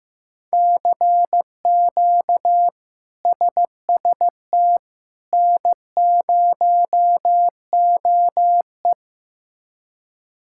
Morse code, heard as a keyed tone, about 16 wpm, 700 Hz. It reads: CQ SST N0OE
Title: K1USN SST QSO Practice - 15wpm